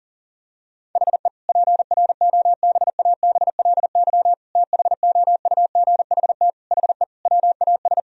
Morse code, heard as a keyed tone, about 40 wpm, 700 Hz. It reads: HE PROBABLY THOUGHT HE WAS
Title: Sentences from Top 1000 Words - Review Part 4 - 40wpm